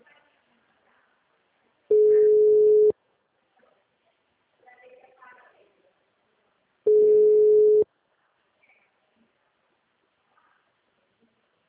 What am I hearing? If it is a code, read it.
Ringback tone (Europe, 425 Hz), with speech.
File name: out- -2019- - - .553